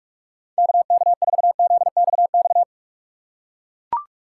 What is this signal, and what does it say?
Morse code, heard as a keyed tone, about 42 wpm, 700 Hz. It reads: KK4ZXX
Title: USA Call Signs - 6 Characters - 6x Character Spacing - 45wpm